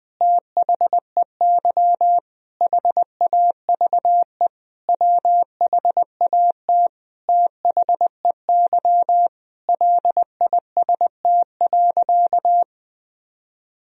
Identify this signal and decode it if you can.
Morse code, heard as a keyed tone, about 21 wpm, 700 Hz. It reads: THEY HAVE WHAT THEY LIST.
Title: Sentences from Top 1000 Words - Review Part 2 - 20wpm